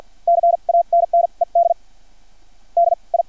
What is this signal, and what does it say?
Morse code, heard as a keyed tone, about 50 wpm, spaced as for 37 wpm, 680 Hz. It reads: CANNED DI
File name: archived / CRSnet